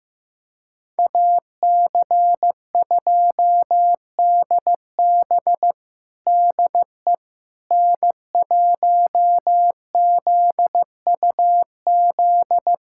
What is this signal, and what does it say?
Morse code, heard as a keyed tone, about 15 wpm, 700 Hz. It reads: AC2DB DE N1ZUZ
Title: USA Callsign DE Callsign - 5 Characters - 15wpm